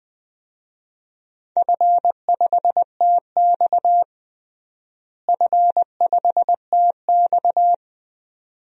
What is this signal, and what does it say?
Morse code, heard as a keyed tone, about 20 wpm, 700 Hz. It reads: F5TX F5TX
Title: DXCC-100 Call Signs - 4 Characters - Repeated 2x - 20wpm